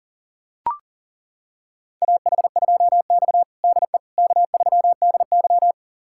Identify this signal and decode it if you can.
Morse code, heard as a keyed tone, about 40 wpm, 700 Hz. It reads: AH2X DE K3DY
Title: USA Callsign DE Callsign - 4 Characters - 40wpm